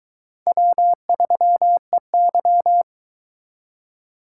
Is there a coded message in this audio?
Yes: Morse code from 0.5 to 2.8 s, heard as a keyed tone, about 23 wpm, 700 Hz. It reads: W3EY